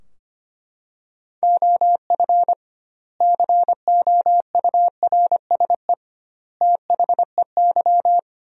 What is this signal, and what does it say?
Morse code, heard as a keyed tone, about 25 wpm, 700 Hz. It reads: OF COURSE THEY
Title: 2x Word-Spacing - Sentences from Top 500 Words - 25wpm